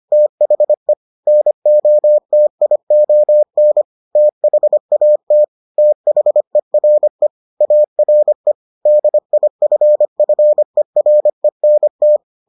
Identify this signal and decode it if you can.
Morse code, heard as a keyed tone, about 25 wpm, 600 Hz. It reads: THE NOTION THAT THERE ARE DIFFERENT